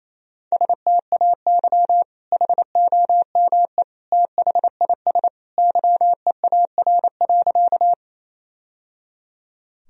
Morse code, heard as a keyed tone, about 28 wpm, 700 Hz. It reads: STAY HOME THIS YEAR.
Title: Sentences from Top 1000 Words - Review Part 1 - 28wpm